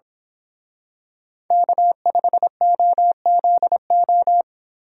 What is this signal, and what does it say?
Morse code, heard as a keyed tone, about 25 wpm, 700 Hz. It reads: K5OZO
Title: Speed Racing - USA 1x3 Call Signs 22wpm